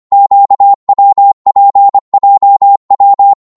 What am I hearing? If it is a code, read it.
Morse code, heard as a keyed tone, about 25 wpm, 800 Hz. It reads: QWPJW
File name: Morse / ltr-25wpm